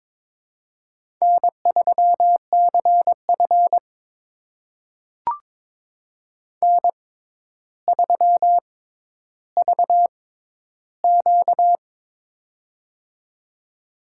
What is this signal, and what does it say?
Morse code, heard as a keyed tone, about 22 wpm, spaced as for 11 wpm, 700 Hz. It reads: N3CF N3VQ
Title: USA Call Signs - 4 Characters - 6x Character Spacing - 22wpm